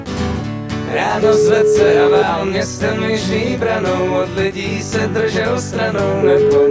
Ringback tone (Europe, 425 Hz), with noise.